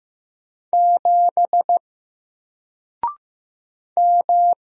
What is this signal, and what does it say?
Morse code, heard as a keyed tone, about 15 wpm, 700 Hz. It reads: 7 M
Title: Single Character (Letter:Number:Punctuation:Prosign) 15wpm